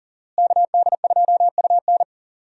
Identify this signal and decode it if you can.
Morse code, heard as a keyed tone, about 40 wpm, 700 Hz. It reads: KD2UN
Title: USA Call Signs - 5 Characters - Repeated 3x - 40wpm